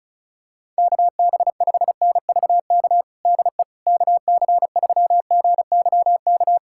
Morse code, heard as a keyed tone, about 34 wpm, 700 Hz. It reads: KB5NVK DE KC3GYK
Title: USA Callsign DE Callsign 35wpm